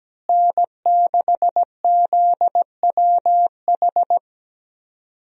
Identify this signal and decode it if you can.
Morse code, heard as a keyed tone, about 17 wpm, 700 Hz. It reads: N6ZWH
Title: USA Call Signs - 5 Characters - Repeated 2x - 17wpm